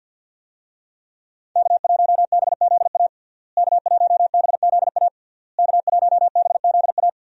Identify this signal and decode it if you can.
Morse code, heard as a keyed tone, about 49 wpm, 700 Hz. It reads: K1BZA K1BZA K1BZA
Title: USA Call Signs - 5 Characters - Repeated 3x - 50wpm